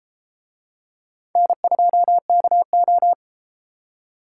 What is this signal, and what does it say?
Morse code, heard as a keyed tone, about 30 wpm, 700 Hz. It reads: N2KO